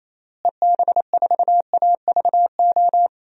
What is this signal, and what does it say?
Morse code, heard as a keyed tone, about 28 wpm, 700 Hz. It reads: EB4AVO
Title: DXCC-100 Call Signs - 6 Characters - Repeated 3x - 28wpm